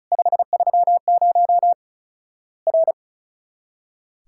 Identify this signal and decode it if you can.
Morse code, heard as a keyed tone, about 36 wpm, 700 Hz. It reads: 530 R
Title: Worked All Europe DX Contest - QTC Exchange - 35wpm